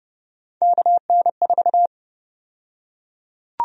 Morse code, heard as a keyed tone, about 30 wpm, 700 Hz. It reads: KN4
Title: USA Call Sign Prefixes - Repeated 2x - 30wpm